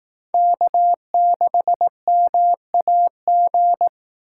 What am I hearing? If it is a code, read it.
Morse code, heard as a keyed tone, about 18 wpm, 700 Hz. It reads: K6MAG
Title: Speed Racing - USA 1x3 Call Signs 15wpm